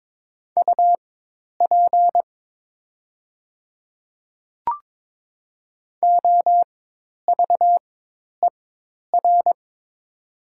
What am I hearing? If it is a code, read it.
Morse code, heard as a keyed tone, about 22 wpm, spaced as for 10 wpm, 700 Hz. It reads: UP OVER
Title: 4x Character Spacing - Top 100 Words - 22wpm